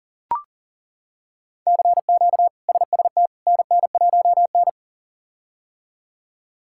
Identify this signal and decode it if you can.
Morse code, heard as a keyed tone, about 37 wpm, 700 Hz. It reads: CQ SST NN1N
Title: K1USN SST QSO Practice - 40wpm